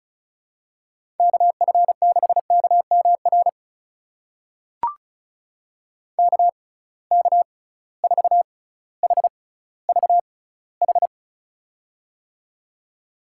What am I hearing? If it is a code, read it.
Morse code, heard as a keyed tone, about 36 wpm, spaced as for 18 wpm, 700 Hz. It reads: KF6KMR KK4HVH
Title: USA Call Signs - 6 Characters - 6x Character Spacing - 35wpm